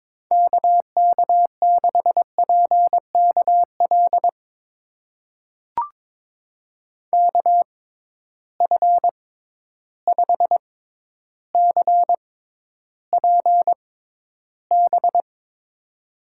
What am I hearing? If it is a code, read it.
Morse code, heard as a keyed tone, about 22 wpm, spaced as for 11 wpm, 700 Hz. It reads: KK6PKL KF5CPB
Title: USA Call Signs - 6 Characters - 6x Character Spacing - 22wpm